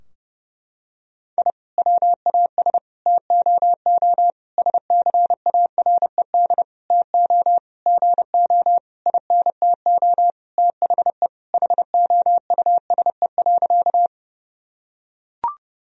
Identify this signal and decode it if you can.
Morse code, heard as a keyed tone, about 32 wpm, 700 Hz. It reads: I WAS TOO SCARED TO GO INTO THE HOUSE.
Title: Sentences from Top 1000 Words - Review Part 4 - 30wpm